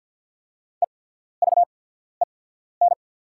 Morse code, heard as a keyed tone, about 48 wpm, spaced as for 14 wpm, 700 Hz. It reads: EVEN
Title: 8x Character Spacing - Top 300 Words - 50wpm